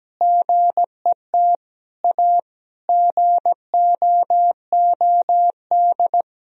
Morse code, heard as a keyed tone, about 17 wpm, 700 Hz. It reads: GET A GOOD
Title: Sentences from Top 1000 Words - Review Part 3 - 17wpm